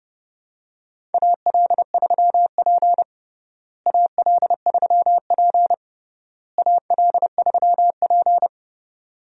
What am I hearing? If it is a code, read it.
Morse code, heard as a keyed tone, about 30 wpm, 700 Hz. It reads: AL3P AL3P AL3P
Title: USA Call Signs - 4 Characters - Repeated 3x - 30wpm